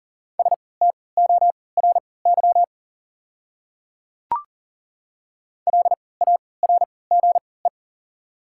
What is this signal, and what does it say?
Morse code, heard as a keyed tone, about 40 wpm, spaced as for 23 wpm, 700 Hz. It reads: STORY LARGE
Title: 3x Character Spacing - Top 500 Words - 40wpm